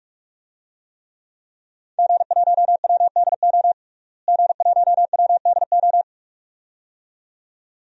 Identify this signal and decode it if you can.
Morse code, heard as a keyed tone, about 41 wpm, 700 Hz. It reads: G1WDO G1WDO
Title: DXCC-100 Call Signs - 5 Characters - Repeated 2x - 45wpm